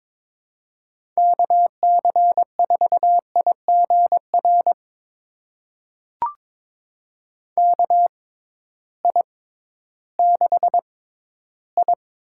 Morse code, heard as a keyed tone, about 22 wpm, spaced as for 13 wpm, 700 Hz. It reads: KC4IGR KI6I
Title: USA Call Signs - 6 Characters - 6x Character Spacing - 22wpm